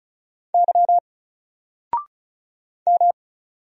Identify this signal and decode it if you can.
Morse code, heard as a keyed tone, about 33 wpm, 700 Hz. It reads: Y M
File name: Single-Letter-Number-35wpm